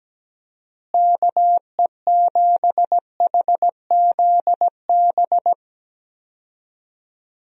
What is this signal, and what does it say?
Morse code, heard as a keyed tone, about 17 wpm, 700 Hz. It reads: KE7HZB
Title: USA Call Signs - 6 Characters - Repeated 2x - 17wpm